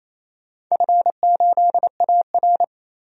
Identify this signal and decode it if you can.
Morse code, heard as a keyed tone, about 28 wpm, 700 Hz. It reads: F8AR
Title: DXCC-100 Call Signs - 4 Characters - 28wpm